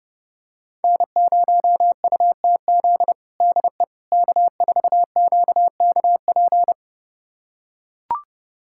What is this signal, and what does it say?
Morse code, heard as a keyed tone, about 32 wpm, 700 Hz. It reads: N0UTZ DE K4QKP
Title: USA Callsign DE Callsign - 5 Characters - 30wpm